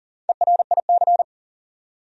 Morse code, heard as a keyed tone, about 40 wpm, 700 Hz. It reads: ERIC
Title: USA Common Names - Max 4 Letters - 40wpm